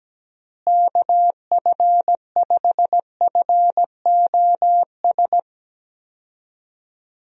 Morse code, heard as a keyed tone, about 17 wpm, 700 Hz. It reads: KF5FOS
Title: USA Call Signs - 6 Characters - Repeated 3x - 17wpm